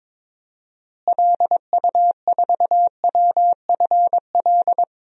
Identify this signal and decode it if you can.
Morse code, heard as a keyed tone, about 22 wpm, 700 Hz. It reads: LU4WFL
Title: DXCC-100 Call Signs - 6 Characters - Repeated 2x - 22wpm